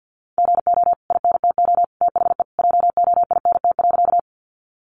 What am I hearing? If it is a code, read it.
Morse code, heard as a keyed tone, about 50 wpm, 700 Hz. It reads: GO INTO THE JOINT.
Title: Sentences from Top 1300 Words 50wpm